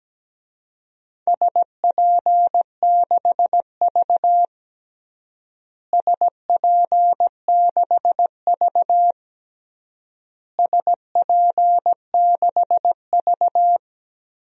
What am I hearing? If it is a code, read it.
Morse code, heard as a keyed tone, about 17 wpm, 700 Hz. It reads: SP6V SP6V SP6V
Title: DXCC-100 Call Signs - 4 Characters - Repeated 3x - 17wpm